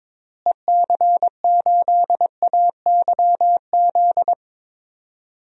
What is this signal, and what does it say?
Morse code, heard as a keyed tone, about 22 wpm, 700 Hz. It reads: EC8AYZ
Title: DXCC-100 Call Signs - 6 Characters - Repeated 3x - 22wpm